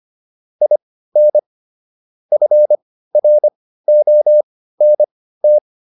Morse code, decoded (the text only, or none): IN FRONT